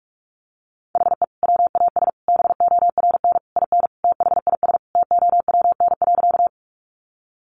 Morse code, heard as a keyed tone, about 44 wpm, 700 Hz. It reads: HE WAS BORN IN THIS TOWN.
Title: Sentences from Top 1000 Words 45wpm